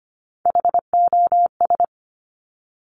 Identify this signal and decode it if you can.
Morse code, heard as a keyed tone, about 25 wpm, 700 Hz. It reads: HOS
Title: Top Three Letter Combinations in Words 25wpm